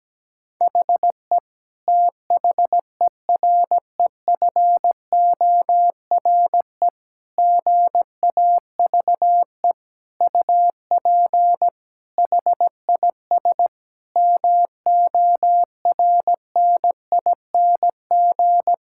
Morse code, heard as a keyed tone, about 17 wpm, 700 Hz. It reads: HE THEREFORE GAVE UP HIS MORNING
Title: Sentences from Top 1000 Words - Review Part 5 - 17wpm